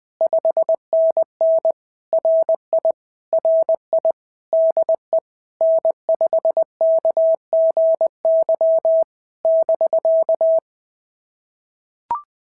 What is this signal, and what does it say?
Morse code, heard as a keyed tone, about 20 wpm, 650 Hz. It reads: 5NN RI RI DE N5KGY <BK>
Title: POTA QSO Practice - 20wpm